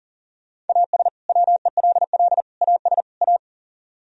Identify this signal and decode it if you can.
Morse code, heard as a keyed tone, about 52 wpm, 700 Hz. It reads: AS WELL AS A